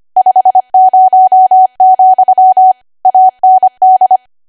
Morse code, heard as a keyed tone, about 25 wpm, 750 Hz. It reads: 50, AND
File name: Archive / 25 WPM / 25WPM